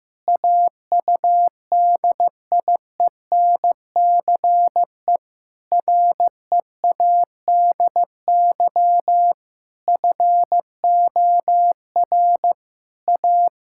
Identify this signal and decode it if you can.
Morse code, heard as a keyed tone, about 15 wpm, 700 Hz. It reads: AUDIENCE READY FOR A
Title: Sentences from Top 1000 Words - Review Part 5 - 15wpm